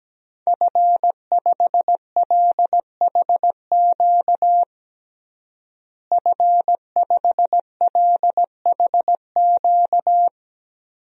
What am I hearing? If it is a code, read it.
Morse code, heard as a keyed tone, about 17 wpm, 700 Hz. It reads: F5LHQ F5LHQ
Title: DXCC-100 Call Signs - 5 Characters - Repeated 3x - 17wpm